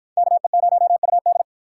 Morse code, heard as a keyed tone, about 49 wpm, 700 Hz. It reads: KE0UD